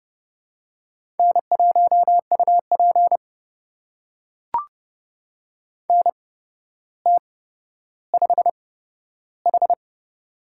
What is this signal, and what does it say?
Morse code, heard as a keyed tone, about 30 wpm, spaced as for 13 wpm, 700 Hz. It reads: N1UP NT5H
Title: USA Call Signs - 8x Character Spacing - 30wpm